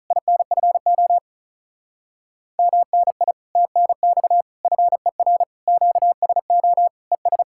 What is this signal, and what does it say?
Morse code, heard as a keyed tone, about 45 wpm, 700 Hz. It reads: INFO MNI TNX FER QSO ES